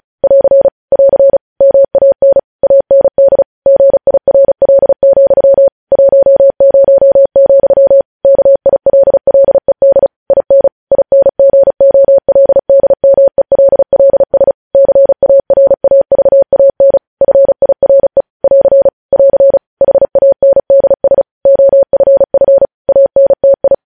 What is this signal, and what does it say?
Morse code, heard as a keyed tone, about 36 wpm, 550 Hz. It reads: <AR> <AR> MAN AND GIRL, 10, KILLED IN INGOLDMELLS CARAVAN FIRE <AR> <AR> HANDS OFF ANTI
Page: News Headlines in Morse Code at 35 WPM